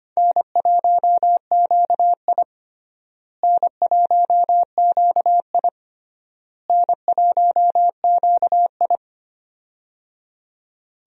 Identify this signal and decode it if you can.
Morse code, heard as a keyed tone, about 25 wpm, 700 Hz. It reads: N1QI N1QI N1QI
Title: USA Call Signs - 4 Characters - Repeated 3x - 25wpm